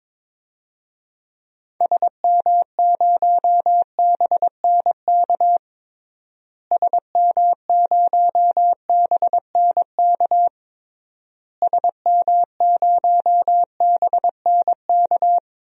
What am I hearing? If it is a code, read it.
Morse code, heard as a keyed tone, about 22 wpm, 700 Hz. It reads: SM0BNK SM0BNK SM0BNK
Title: DXCC-100 Call Signs - 6 Characters - Repeated 3x - 22wpm